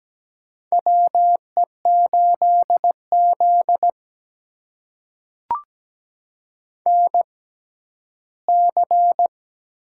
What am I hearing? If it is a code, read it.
Morse code, heard as a keyed tone, about 17 wpm, spaced as for 12 wpm, 700 Hz. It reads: WE8Z NC
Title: USA Call Signs - 6x Character Spacing - 17wpm